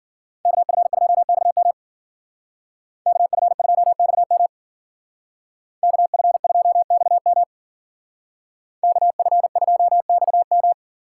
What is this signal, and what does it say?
Morse code, heard as a keyed tone, about 48 wpm, 700 Hz. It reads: KF2XM KF2XM KF2XM KF2XM
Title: Speed Racing - USA 2x2 Call Signs 40wpm